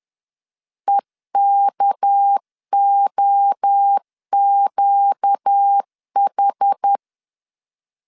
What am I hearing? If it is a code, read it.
Morse code, heard as a keyed tone, about 10 wpm, 800 Hz. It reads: EKOQH
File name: LEZIONI / 6 PKX0 / 6 PKX0 10wpm 2707